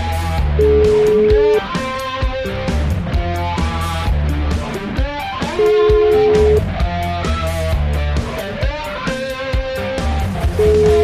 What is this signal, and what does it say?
Ringback tone (Europe, 425 Hz), with noise.